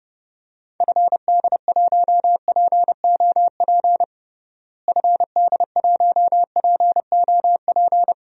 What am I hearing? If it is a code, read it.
Morse code, heard as a keyed tone, about 32 wpm, 700 Hz. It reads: FD1POP FD1POP
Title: DXCC-100 Call Signs - 6 Characters - Repeated 2x - 30wpm